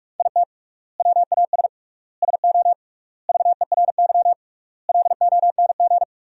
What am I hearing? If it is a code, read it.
Morse code, heard as a keyed tone, about 46 wpm, 700 Hz. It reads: IT WAS SO VERY LONG
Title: 3x Word-Spacing - Sentences from Top 500 Words - 45wpm